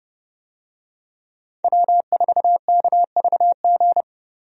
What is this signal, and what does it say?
Morse code, heard as a keyed tone, about 30 wpm, 700 Hz. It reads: W4KVG
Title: USA Call Signs - 5 Characters - 30wpm